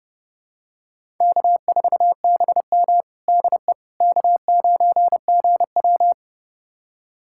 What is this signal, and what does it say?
Morse code, heard as a keyed tone, about 32 wpm, 700 Hz. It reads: K4BM DE K9GW
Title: USA Callsign DE Callsign - 4 Characters - 30wpm